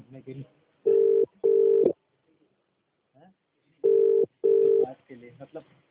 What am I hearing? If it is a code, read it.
Ringback tone (UK), 400 + 450 Hz, with speech.